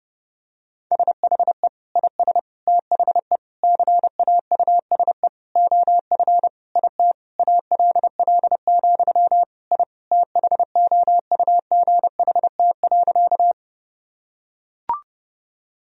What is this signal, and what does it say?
Morse code, heard as a keyed tone, about 32 wpm, 700 Hz. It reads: SHE IS THE CAUSE OF IT ALL, I THOUGHT.
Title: Sentences from Top 1000 Words - Review Part 1 - 30wpm